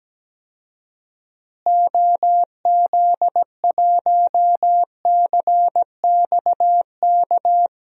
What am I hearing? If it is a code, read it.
Morse code, heard as a keyed tone, about 17 wpm, 700 Hz. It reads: OZ1CXK
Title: DXCC-100 Call Signs - 6 Characters - Repeated 3x - 17wpm